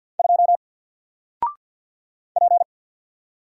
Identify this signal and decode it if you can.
Morse code, heard as a keyed tone, about 48 wpm, 700 Hz. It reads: 2 P
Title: Single Letter-Number Alt 50wpm